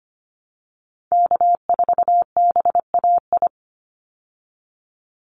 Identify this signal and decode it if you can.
Morse code, heard as a keyed tone, about 25 wpm, 700 Hz. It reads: K4BAI
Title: Call Signs Repeated 3x - 25wpm